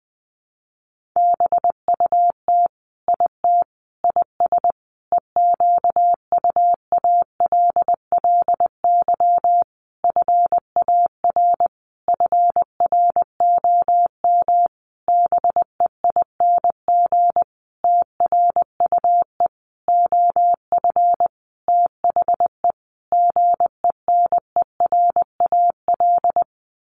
Morse code, heard as a keyed tone, about 20 wpm, 700 Hz. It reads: BUT IT IS EQUALLY FAR FROM BEING TRUE OF THE GENERAL
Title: Sentences from Top 1400 Words 20wpm